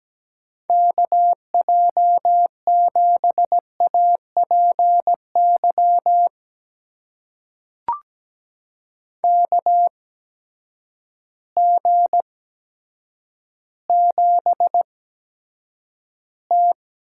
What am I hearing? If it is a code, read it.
Morse code, heard as a keyed tone, about 17 wpm, spaced as for 8 wpm, 700 Hz. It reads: KJ7APY KG7T
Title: USA Call Signs - 8x Character Spacing - 17wpm